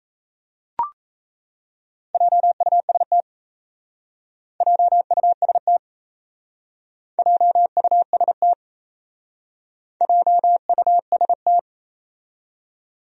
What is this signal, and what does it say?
Morse code, heard as a keyed tone, about 36 wpm, 700 Hz. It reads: JUST JUST JUST JUST